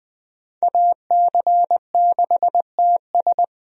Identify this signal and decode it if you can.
Morse code, heard as a keyed tone, about 20 wpm, 700 Hz. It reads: AC6TS
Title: USA Call Signs - 5 Characters - Repeated 2x - 20wpm